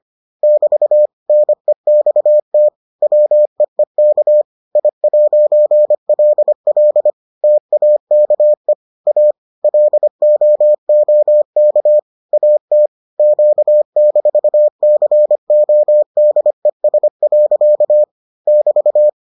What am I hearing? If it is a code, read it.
Morse code, heard as a keyed tone, about 25 wpm, 600 Hz. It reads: <BT> NEXT WEEK I'LL TAKE A LOOK AT Q-CODES. <BT>